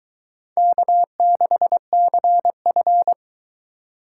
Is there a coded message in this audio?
Yes: Morse code from 0.6 to 3.1 s, heard as a keyed tone, about 20 wpm, 700 Hz. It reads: K6CF